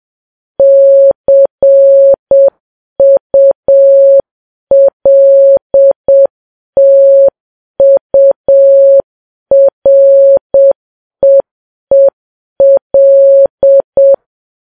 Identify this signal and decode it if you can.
Morse code, heard as a keyed tone, about 7 wpm, 550 Hz. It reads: CULTUREEL
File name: Files Web tls / Morse on9fv / morse7kps